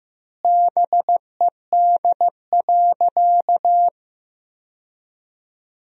Morse code, heard as a keyed tone, about 16 wpm, 700 Hz. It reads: BED.
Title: 4x Word-Spacing - Sentences from Top 500 Words - 15wpm